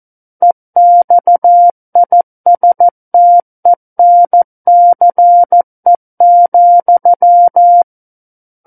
Morse code, heard as a keyed tone, about 14 wpm, 700 Hz. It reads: EXISTENCE,
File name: Book One / 05-Chapter 5 - 14 WPM